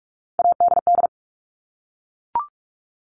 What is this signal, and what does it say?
Morse code, heard as a keyed tone, about 44 wpm, 700 Hz. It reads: ADD
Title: Top 200 Words 45wpm